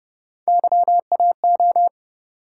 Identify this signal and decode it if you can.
Morse code, heard as a keyed tone, about 29 wpm, 700 Hz. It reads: YAO